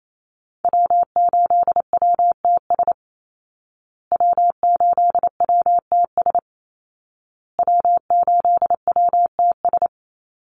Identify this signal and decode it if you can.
Morse code, heard as a keyed tone, about 28 wpm, 700 Hz. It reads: W8WTS W8WTS W8WTS
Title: Call Signs Repeated 2x - 28wpm